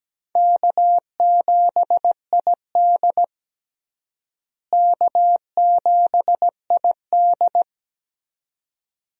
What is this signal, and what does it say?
Morse code, heard as a keyed tone, about 17 wpm, 700 Hz. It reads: K7ID K7ID
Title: USA Call Signs - 4 Characters - Repeated 2x - 17wpm